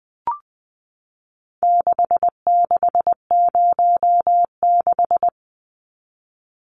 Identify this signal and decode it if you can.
Morse code, heard as a keyed tone, about 20 wpm, 700 Hz. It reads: 6606
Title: Numbers - 4-Digit 20wpm